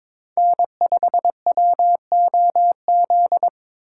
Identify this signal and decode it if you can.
Morse code, heard as a keyed tone, about 26 wpm, 700 Hz. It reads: N5WOZ
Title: Speed Racing - USA 1x3 Call Signs 22wpm